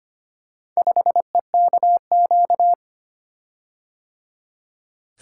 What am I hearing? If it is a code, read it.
Morse code, heard as a keyed tone, about 25 wpm, 700 Hz. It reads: 5EKQ
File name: practice / 25wpm / Four Letter-Number Combinations - 25wpm